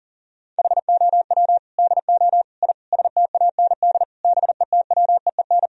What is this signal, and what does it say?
Morse code, heard as a keyed tone, about 40 wpm, 700 Hz. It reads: HOW DO I STAND BETWEEN
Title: Sentences from Top 1000 Words - Review Part 1 - 40wpm